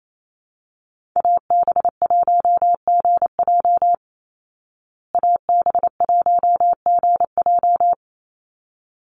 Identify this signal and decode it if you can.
Morse code, heard as a keyed tone, about 28 wpm, 690 Hz. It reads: AB1GJ AB1GJ
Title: Call Signs Repeated 3x - 28wpm